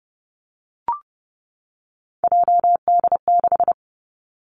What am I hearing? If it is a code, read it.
Morse code, heard as a keyed tone, about 30 wpm, 700 Hz. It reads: JD6
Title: Three Challenging Letter-Number Combinations 30wpm